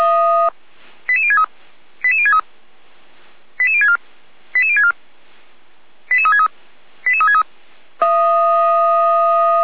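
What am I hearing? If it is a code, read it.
Five-tone selective call (ZVEI), heard as quick runs of five tones, tones of about 70 ms. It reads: 88063 88063 88064 88064 88363 88363
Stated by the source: ZVEI five-tone selective call received by SDR, on the feuerwehr-orlamuende fire brigade audio server.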